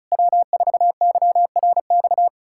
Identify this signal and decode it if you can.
Morse code, heard as a keyed tone, about 40 wpm, 700 Hz. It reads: W4YRX